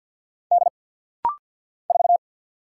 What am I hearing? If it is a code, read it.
Morse code, heard as a keyed tone, about 50 wpm, 690 Hz. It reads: D 4
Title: Single Letter-Number - Rapid-Fire 50wpm